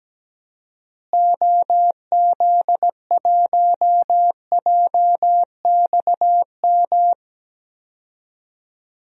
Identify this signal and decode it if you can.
Morse code, heard as a keyed tone, about 17 wpm, 700 Hz. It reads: OZ1JXM
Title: DXCC-100 Call Signs - 6 Characters - Repeated 2x - 17wpm